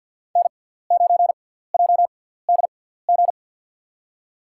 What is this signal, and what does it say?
Morse code, heard as a keyed tone, about 53 wpm, spaced as for 17 wpm, 700 Hz. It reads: N9JDG
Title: USA Call Signs - 5 Characters - 6x Character Spacing - 50wpm